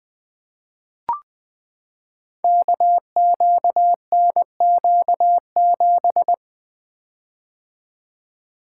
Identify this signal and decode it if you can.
Morse code, heard as a keyed tone, about 20 wpm, 700 Hz. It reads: KQNQ7
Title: Code Groups - 5 Letter:Numbers - 20wpm